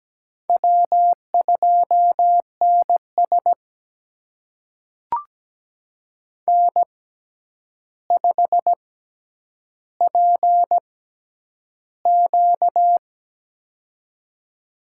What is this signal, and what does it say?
Morse code, heard as a keyed tone, about 17 wpm, spaced as for 9 wpm, 700 Hz. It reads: W2NS N5PQ
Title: USA Call Signs - 4 Characters - 6x Character Spacing - 17wpm